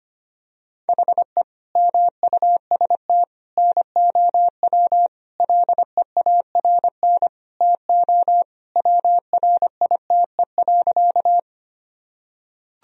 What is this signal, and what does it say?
Morse code, heard as a keyed tone, about 25 wpm, 700 Hz. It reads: HE MUST NOW LEARN TO WRITE.
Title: Sentences from Top 1000 Words - Review Part 1 - 25wpm